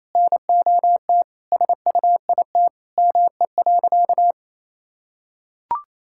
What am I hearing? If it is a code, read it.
Morse code, heard as a keyed tone, about 28 wpm, 700 Hz. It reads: NOT SUIT ME.
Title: Sentences from Top 1600 Words 28wpm